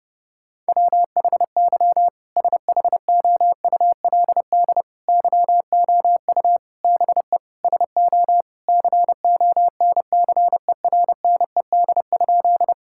Morse code, heard as a keyed tone, about 30 wpm, 700 Hz. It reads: WHY SHOULD YOU BE SO CONCERNED?
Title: Sentences from Top 1000 Words - Review Part 6 - 30wpm